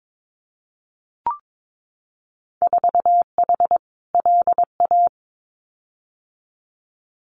Morse code, heard as a keyed tone, about 22 wpm, 700 Hz. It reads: 4H LA
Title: Winter Field Day Exchanges Encore 22wpm